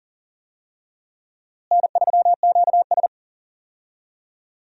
Morse code, heard as a keyed tone, about 40 wpm, 700 Hz. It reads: N3QS
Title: USA Call Signs - 4 Characters - Repeated 3x - 40wpm